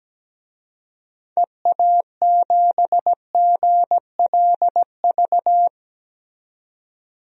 Morse code, heard as a keyed tone, about 17 wpm, 700 Hz. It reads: EA7GLV
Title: DXCC-100 Call Signs - 6 Characters - 17wpm